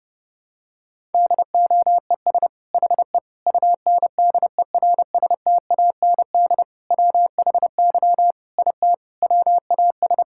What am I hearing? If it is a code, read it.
Morse code, heard as a keyed tone, about 32 wpm, 700 Hz. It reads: DOES HE UNDERSTAND WHY IT WAS
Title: Sentences from Top 1000 Words - Review Part 5 - 30wpm